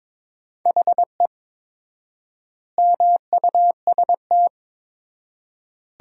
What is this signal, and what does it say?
Morse code, heard as a keyed tone, about 22 wpm, 700 Hz. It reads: HE MUST
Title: 4x Word-Spacing - Sentences from Top 500 Words - 22wpm